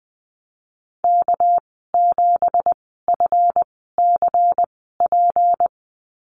Morse code, heard as a keyed tone, about 20 wpm, spaced as for 15 wpm, 700 Hz. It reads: K7FCP
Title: 200 Lesson 32g - Letter k - Call Signs 20wpm